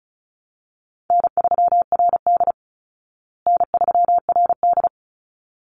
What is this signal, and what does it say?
Morse code, heard as a keyed tone, about 34 wpm, 700 Hz. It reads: N3RD N3RD
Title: Easy Call Signs Repeated 2x - 35wpm